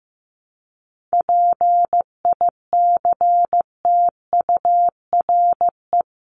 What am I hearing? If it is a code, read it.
Morse code, heard as a keyed tone, about 15 wpm, 700 Hz. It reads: PICTURE
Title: Top 500 Words 15wpm